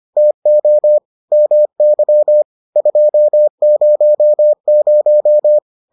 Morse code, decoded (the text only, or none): TO MY 200